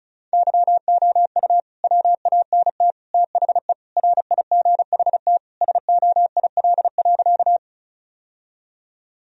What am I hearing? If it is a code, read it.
Morse code, heard as a keyed tone, about 34 wpm, 700 Hz. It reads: YOU WANT THE RIGHT SOIL.
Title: Sentences from Top 1000 Words - Review Part 4 - 35wpm